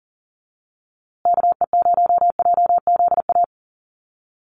Morse code, heard as a keyed tone, about 40 wpm, 700 Hz. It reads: KE0JZA
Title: USA Call Signs Repeated 2x 40wpm